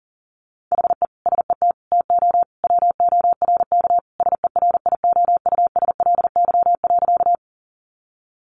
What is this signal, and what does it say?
Morse code, heard as a keyed tone, about 40 wpm, 700 Hz. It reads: HE SET TO WORK SERIOUSLY.
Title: Sentences from Top 1200 Words 40wpm